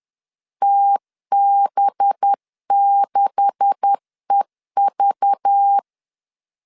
Morse code, heard as a keyed tone, about 10 wpm, 800 Hz. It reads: TB6EV
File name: LEZIONI / 10 SIGNS / 10D COMA 10wpm 3013